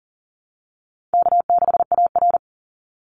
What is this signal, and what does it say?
Morse code, heard as a keyed tone, about 40 wpm, 700 Hz. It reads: K6AR
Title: Easy Call Signs - 40wpm